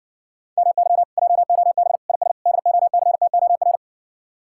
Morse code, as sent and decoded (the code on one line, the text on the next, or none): -- -.-- / .--- --- -... / .. ... / -.. --- -.-. - --- .-.
MY JOB IS DOCTOR